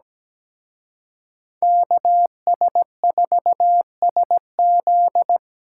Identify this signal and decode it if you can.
Morse code, heard as a keyed tone, about 20 wpm, 700 Hz. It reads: KS4SZ
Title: Speed Racing - USA 2x2 Call Signs 17wpm